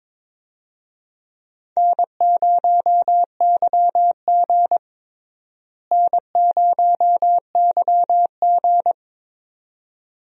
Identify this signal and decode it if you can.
Morse code, heard as a keyed tone, about 22 wpm, 700 Hz. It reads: N0YG N0YG
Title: USA Call Signs - 4 Characters - Repeated 2x - 22wpm